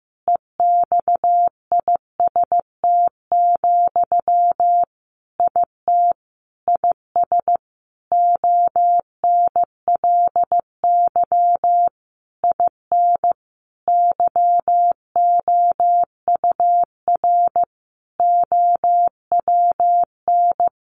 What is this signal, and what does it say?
Morse code, heard as a keyed tone, about 15 wpm, 700 Hz. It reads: EXIST, IT IS ONLY IN YOUR OWN
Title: Sentences from Top 1300 Words 15wpm